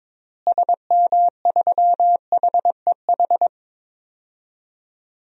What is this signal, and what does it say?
Morse code, heard as a keyed tone, about 22 wpm, 700 Hz. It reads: SM3HEH
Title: DXCC-100 Call Signs - 6 Characters - Repeated 2x - 22wpm